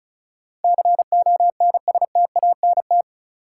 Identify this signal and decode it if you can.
Morse code, heard as a keyed tone, about 36 wpm, 700 Hz. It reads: CONSTANT
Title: Sentences from Top 1500 Words - New Word Drill - 35wpm